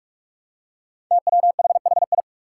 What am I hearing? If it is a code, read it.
Morse code, heard as a keyed tone, about 46 wpm, 700 Hz. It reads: TWHHI
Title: Five Challenging Letter-Number Combinations 45wpm